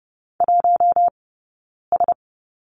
Morse code, heard as a keyed tone, about 30 wpm, spaced as for 9 wpm, 690 Hz. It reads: 1S
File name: course / 30:10wpm / 086 Lesson 15d - Number 5 - Two Characters 30wpm